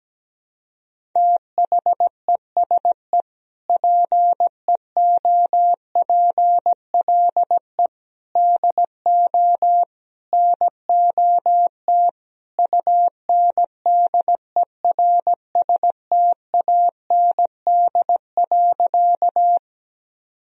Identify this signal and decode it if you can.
Morse code, heard as a keyed tone, about 17 wpm, 700 Hz. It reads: THESE PEOPLE DO NOT UNDERSTAND.
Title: Sentences from Top 1000 Words - Review Part 1 - 17wpm